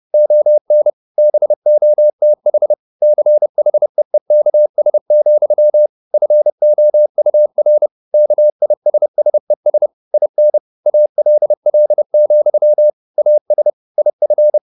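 Morse code, heard as a keyed tone, about 30 wpm, 600 Hz. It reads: ON BOTH CHEEKS, FOUR KISSES IN ALL, AS IF
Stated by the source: Morse code at 30 wpm, the novel Never by Ken Follett in English, from CW med Gnister.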